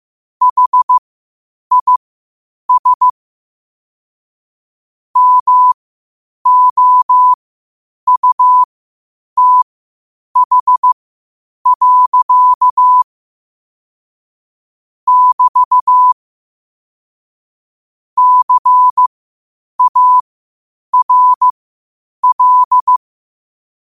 Morse code, heard as a keyed tone, about 15 wpm, spaced as for 8 wpm, 1000 Hz. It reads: HIS MOUTH. <BT> CARL
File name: qotdfiles / 8wpm